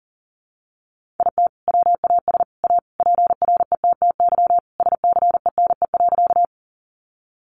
Morse code, heard as a keyed tone, about 40 wpm, 700 Hz. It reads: IT WAS A PRETTY SCENE.
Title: Sentences from Top 1100 Words 40wpm